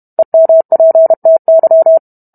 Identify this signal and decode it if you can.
Morse code, heard as a keyed tone, about 32 wpm, 650 Hz. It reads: EMPTY